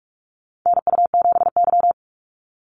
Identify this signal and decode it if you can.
Morse code, heard as a keyed tone, about 44 wpm, 700 Hz. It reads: NU7Y
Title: NA SOTA Chaser Callsigns 45wpm